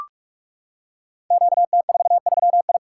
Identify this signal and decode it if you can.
Morse code, heard as a keyed tone, about 44 wpm, 690 Hz. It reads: QT43I
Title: Code Groups - 5 Letter:Numbers - 45wpm